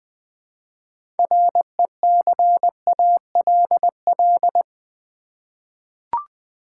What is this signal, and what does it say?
Morse code, heard as a keyed tone, about 20 wpm, 700 Hz. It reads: RECALL
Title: 6 Letter Words - First 3 Letters Given - 20wpm